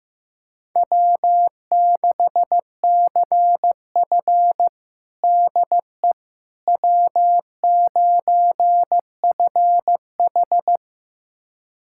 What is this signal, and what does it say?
Morse code, heard as a keyed tone, about 15 wpm, 700 Hz. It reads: W6CF DE W9FH
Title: USA Callsign DE Callsign - 4 Characters - 15wpm